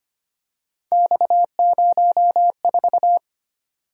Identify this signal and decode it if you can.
Morse code, heard as a keyed tone, about 29 wpm, 700 Hz. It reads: X04
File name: Speed-Racing - Three Letter-Number Combinations - 25wpm